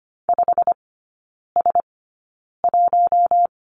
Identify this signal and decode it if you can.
Morse code, heard as a keyed tone, about 25 wpm, spaced as for 9 wpm, 700 Hz. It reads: 5S1
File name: course / 25:10wpm / 093 Lesson 16e - Letter c - Three Characters 25wpm